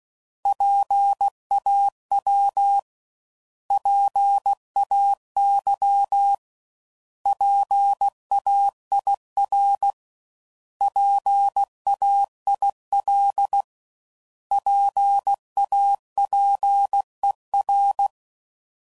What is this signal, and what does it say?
Morse code, heard as a keyed tone, about 16 wpm, 800 Hz. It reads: PAW PAY PAIR PAIL PAPER
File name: Prefixes / By First Letter / Ps